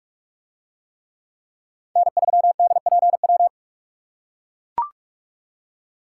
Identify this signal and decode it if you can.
Morse code, heard as a keyed tone, about 41 wpm, 700 Hz. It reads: N3DPW